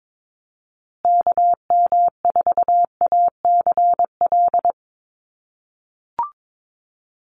Morse code, heard as a keyed tone, about 22 wpm, 700 Hz. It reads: KM4ACL
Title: USA Call Signs Repeated 3x 22wpm